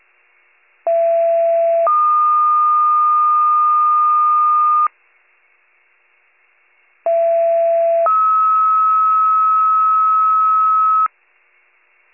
Two-tone page, heard as a one-second tone, then a three-second tone, about 670 then 1220 Hz, about 670 then 1290 Hz. It is heard from 0.9 to 11.1 s.